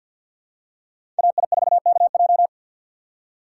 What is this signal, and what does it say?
Morse code, heard as a keyed tone, about 52 wpm, 700 Hz. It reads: AI4KJ
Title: USA Call Signs - 2x Character Spacing - 50wpm